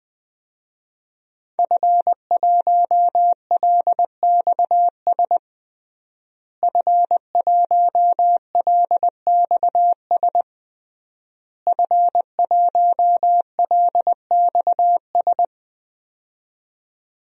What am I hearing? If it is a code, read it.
Morse code, heard as a keyed tone, about 20 wpm, 700 Hz. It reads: F1LXS F1LXS F1LXS
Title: DXCC-100 Call Signs - 5 Characters - Repeated 3x - 20wpm